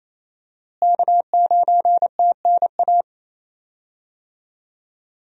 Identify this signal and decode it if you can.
Morse code, heard as a keyed tone, about 28 wpm, 700 Hz. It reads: K9TNA
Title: USA Call Signs - 5 Characters - Repeated 3x - 28wpm